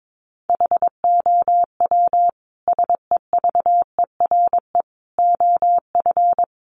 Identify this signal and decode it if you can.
Morse code, heard as a keyed tone, about 22 wpm, 700 Hz. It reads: HOW SEVERE OF
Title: Sentences from Top 1400 Words 22wpm